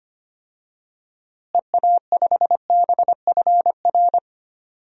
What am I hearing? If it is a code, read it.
Morse code, heard as a keyed tone, about 25 wpm, 700 Hz. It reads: EA5BFR
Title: DXCC-100 Call Signs - 6 Characters - Repeated 3x - 25wpm